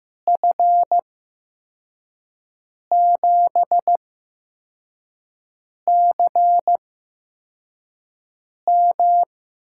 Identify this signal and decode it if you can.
Morse code, heard as a keyed tone, about 15 wpm, spaced as for 4 wpm, 700 Hz. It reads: F7CM
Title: USA Call Signs - 5 Characters - 8x Character Spacing - 15wpm